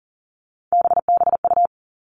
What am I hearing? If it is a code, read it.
Morse code, heard as a keyed tone, about 38 wpm, 700 Hz. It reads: BBU
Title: Three Challenging Letter-Number Combinations 40wpm